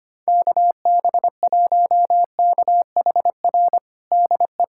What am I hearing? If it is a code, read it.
Morse code, heard as a keyed tone, about 25 wpm, 700 Hz. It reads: KB1KHR DE
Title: USA Callsign DE Callsign - 6 Characters - 25wpm